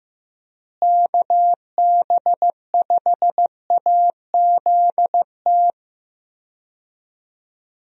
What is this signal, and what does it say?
Morse code, heard as a keyed tone, about 15 wpm, 700 Hz. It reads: KB5AZT
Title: USA Call Signs - 6 Characters - Repeated 3x - 15wpm